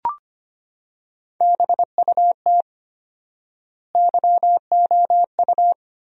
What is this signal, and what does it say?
Morse code, heard as a keyed tone, about 25 wpm, 700 Hz. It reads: BUT YOU
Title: 4x Word-Spacing - Sentences from Top 200 Words - 25wpm